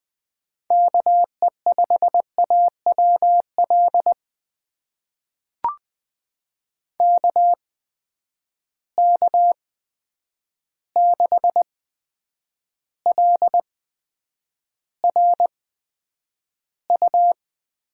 Morse code, heard as a keyed tone, about 20 wpm, spaced as for 9 wpm, 700 Hz. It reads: KE5AWL KK6LRU
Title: USA Call Signs - 8x Character Spacing - 20wpm